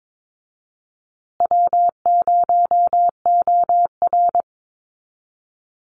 Morse code, heard as a keyed tone, about 22 wpm, 700 Hz. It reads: W0OR
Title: Easy Call Signs Repeated 3x - 22wpm